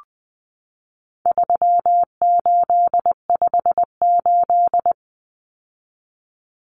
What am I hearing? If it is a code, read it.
Morse code, heard as a keyed tone, about 20 wpm, 700 Hz. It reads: 3858
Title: Common Cut Numbers - 4-Digit 20wpm